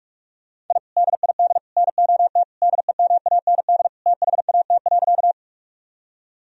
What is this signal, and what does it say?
Morse code, heard as a keyed tone, about 46 wpm, 700 Hz. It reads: I DID NOT DEMAND THAT.
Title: Sentences from Top 1000 Words - Review Part 4 - 45wpm